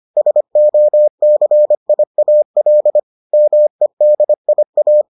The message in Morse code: SOCIAL MEDIA

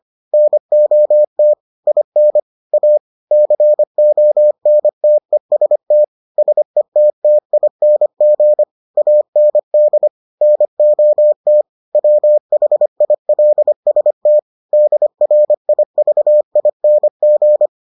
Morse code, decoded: NOT IN A CONTEST SETTING AND NOT WHILST DRIVING

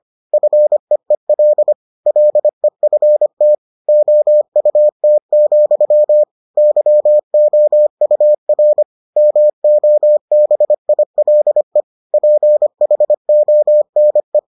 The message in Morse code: FEEL LEFT OUT, YOUR MOBILE PHONE

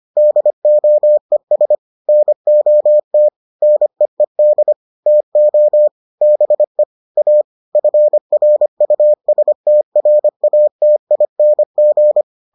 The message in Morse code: DOES NOT NEED TO BE A FRUSTRATING

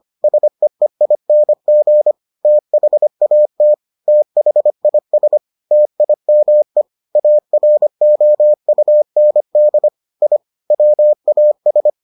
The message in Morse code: SEEING THAT THIS TIME AROUND I WAS

